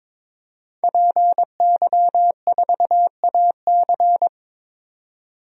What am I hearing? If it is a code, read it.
Morse code, heard as a keyed tone, about 21 wpm, 700 Hz. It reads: PY4AC
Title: DXCC-100 Call Signs - 5 Characters - 22wpm